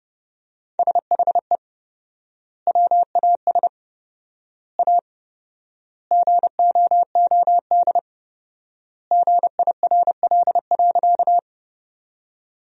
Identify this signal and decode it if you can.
Morse code, heard as a keyed tone, about 30 wpm, 700 Hz. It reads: SHE WAS A GOOD GIRL.
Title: 4x Word-Spacing - Sentences from Top 400 Words - 30wpm